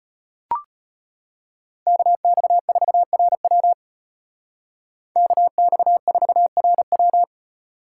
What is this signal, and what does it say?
Morse code, heard as a keyed tone, about 33 wpm, 700 Hz. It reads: KX4RW KX4RW